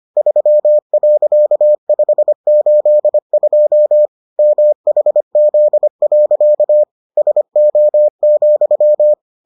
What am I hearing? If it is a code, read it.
Morse code, heard as a keyed tone, about 25 wpm, 600 Hz. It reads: 3.582 MHZ. SO,